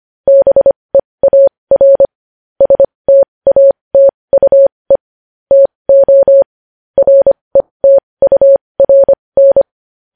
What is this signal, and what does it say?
Morse code, heard as a keyed tone, about 25 wpm, spaced as for 20 wpm, 550 Hz. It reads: BEAR STATUE TO RETURN
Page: news headlines in morse code at 20 wpm